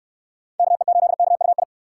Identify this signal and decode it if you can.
Morse code, heard as a keyed tone, about 61 wpm, 700 Hz. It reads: KE8KUI